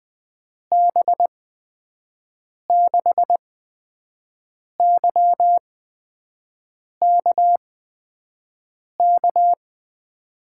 Morse code, heard as a keyed tone, about 20 wpm, spaced as for 5 wpm, 700 Hz. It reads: B6YKK
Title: USA Call Signs - 6 Characters - 8x Character Spacing - 20wpm